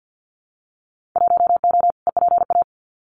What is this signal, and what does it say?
Morse code, heard as a keyed tone, about 50 wpm, 700 Hz. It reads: 1O EPA
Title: Winter Field Day Exchanges 50wpm